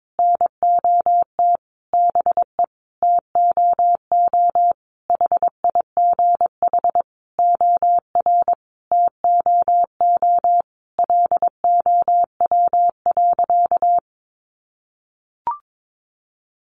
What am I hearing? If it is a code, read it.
Morse code, heard as a keyed tone, about 22 wpm, 700 Hz. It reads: NOT BE TOO HIGH OR TOO LOW.
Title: Sentences from Top 1200 Words 22wpm